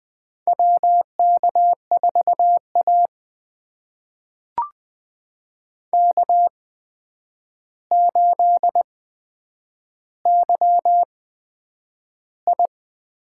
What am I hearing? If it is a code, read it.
Morse code, heard as a keyed tone, about 20 wpm, spaced as for 9 wpm, 700 Hz. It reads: WK4A K8YI
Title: USA Call Signs - 4 Characters - 8x Character Spacing - 20wpm